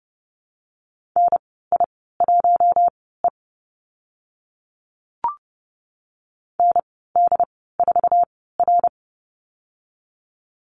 Morse code, heard as a keyed tone, about 30 wpm, spaced as for 17 wpm, 700 Hz. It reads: NI1E ND4R
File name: Lesson 14g - Number 2 - Call Signs 30wpm